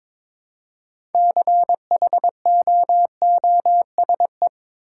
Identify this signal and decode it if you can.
Morse code, heard as a keyed tone, about 22 wpm, 690 Hz. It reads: CHOOSE
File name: practice / 22wpm / Easy Sentences with Hints - 22wpm